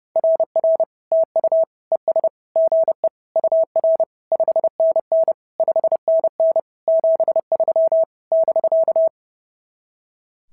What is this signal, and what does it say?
Morse code, heard as a keyed tone, about 28 wpm, 660 Hz. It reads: RR TU ES GE UR 5NN 5NN 73 <BK>
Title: SOTA QSO Practice 30wpm